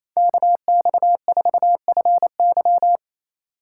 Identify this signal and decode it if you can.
Morse code, heard as a keyed tone, about 28 wpm, 700 Hz. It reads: KX4FY
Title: USA Call Signs - 5 Characters - 4x Character Spacing - 28wpm